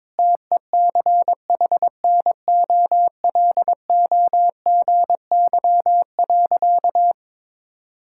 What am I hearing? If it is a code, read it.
Morse code, heard as a keyed tone, about 22 wpm, 700 Hz. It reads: TECHNOLOGY.
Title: Sentences from Top 1000 Words - Review Part 5 - 22wpm